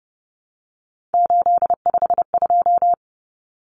Morse code, heard as a keyed tone, about 30 wpm, 700 Hz. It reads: 852
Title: Common Cut Numbers - 3-Digit 30wpm